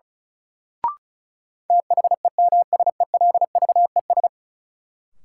Morse code, heard as a keyed tone, about 36 wpm, 700 Hz. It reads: THEMSELVES
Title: Sentences from Top 800 Words - New Word Drill - 35wpm